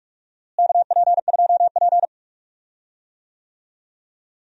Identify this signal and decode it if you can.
Morse code, heard as a keyed tone, about 46 wpm, 700 Hz. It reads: KP2P
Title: DXCC-100 Call Signs - 4 Characters - Repeated 2x - 45wpm